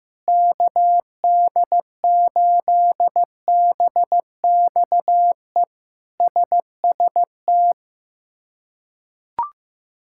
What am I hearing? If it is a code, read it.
Morse code, heard as a keyed tone, about 15 wpm, 700 Hz. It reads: KD8BXE SST
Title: K1USN SST QSO Practice - 15wpm